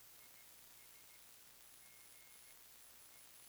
Morse code, heard as a keyed tone, about 15 wpm, spaced as for 11 wpm, 2150 Hz, with noise. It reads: IS GE
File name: sdr / WINRAD WB5RVZ PSK Sample 40m